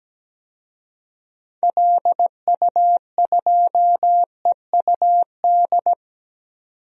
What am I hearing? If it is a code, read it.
Morse code, heard as a keyed tone, about 17 wpm, 700 Hz. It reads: LU2EUD
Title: DXCC-100 Call Signs - 6 Characters - 17wpm